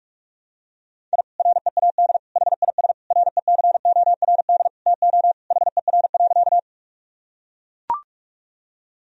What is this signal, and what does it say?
Morse code, heard as a keyed tone, about 41 wpm, 700 Hz. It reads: I READ HIS RECORD TO HER.
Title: Sentences from Top 1000 Words - Review Part 3 - 45wpm